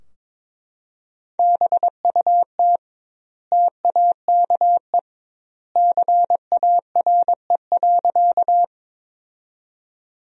Morse code, heard as a keyed tone, about 21 wpm, 700 Hz. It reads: BUT TAKE CARE.
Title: 2x Word-Spacing - Sentences from Top 300 Words - 22wpm